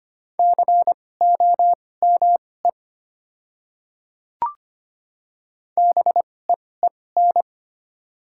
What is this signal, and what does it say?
Morse code, heard as a keyed tone, about 25 wpm, spaced as for 18 wpm, 700 Hz. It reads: COME BEEN
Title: 2x Character Spacing - Top 100 Words - 25wpm